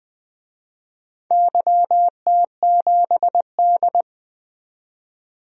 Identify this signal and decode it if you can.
Morse code, heard as a keyed tone, about 20 wpm, 700 Hz. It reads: YT7D
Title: DXCC-100 Call Signs - 4 Characters - 20wpm